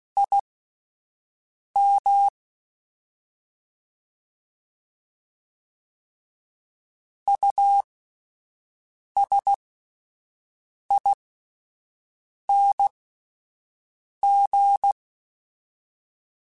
Morse code, heard as a keyed tone, about 16 wpm, spaced as for 5 wpm, 800 Hz. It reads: IM USING